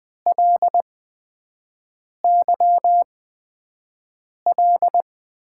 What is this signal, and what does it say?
Morse code, heard as a keyed tone, about 20 wpm, spaced as for 5 wpm, 700 Hz. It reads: LYL
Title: USA Call Signs - 8x Character Spacing - 20wpm